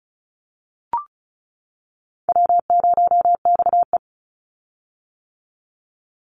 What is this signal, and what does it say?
Morse code, heard as a keyed tone, about 34 wpm, 700 Hz. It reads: W0XE
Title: Call Signs - 35wpm